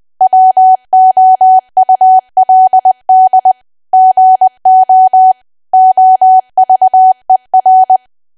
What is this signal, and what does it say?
Morse code, heard as a keyed tone, about 20 wpm, 750 Hz. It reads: WOULD GO OVER
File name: Archive / 20 WPM / 20WPM